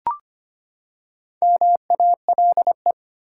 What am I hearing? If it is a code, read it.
Morse code, heard as a keyed tone, about 25 wpm, 700 Hz. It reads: MALE
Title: Four Letter Words - 25wpm